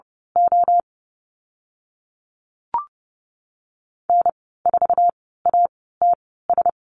Morse code, heard as a keyed tone, about 30 wpm, spaced as for 17 wpm, 700 Hz. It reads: O N4ATS
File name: Lesson 15g - Number 5 - Call Signs 30wpm